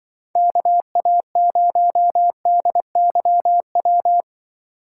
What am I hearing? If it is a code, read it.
Morse code, heard as a keyed tone, about 22 wpm, 700 Hz. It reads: KA0DYW
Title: Speed Racing - USA Call Signs 20wpm